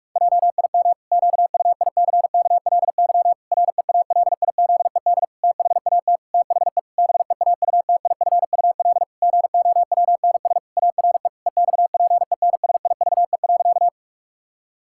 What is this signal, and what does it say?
Morse code, heard as a keyed tone, about 46 wpm, 700 Hz. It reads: JIM QUICKLY REALIZED THAT THE BEAUTIFUL GOWNS ARE EXPENSIVE.
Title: Pangram Challenge v2 - 45wpm